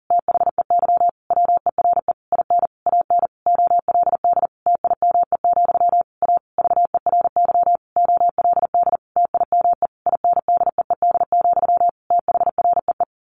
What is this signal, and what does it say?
Morse code, heard as a keyed tone, about 40 wpm, 700 Hz. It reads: THEY WERE IN AN OLD TIME, A VERY OLD TIME INDEED, THREE
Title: A Sound Of Thunder 40wpm